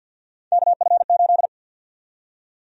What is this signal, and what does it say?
Morse code, heard as a keyed tone, about 53 wpm, 700 Hz. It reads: KF8
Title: USA Call Sign Prefixes - 50wpm